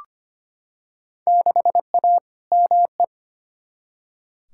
Morse code, heard as a keyed tone, about 25 wpm, 700 Hz. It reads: 6A ME
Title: Field Day Exchanges - 25wpm